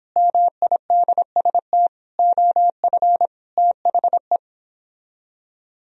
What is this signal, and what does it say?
Morse code, heard as a keyed tone, about 29 wpm, 700 Hz. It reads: MIDST OF THE